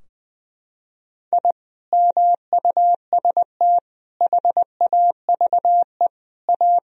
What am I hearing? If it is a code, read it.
Morse code, heard as a keyed tone, about 20 wpm, 700 Hz. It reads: I MUST HAVE A
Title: Sentences from Top 1000 Words - Review Part 4 - 20wpm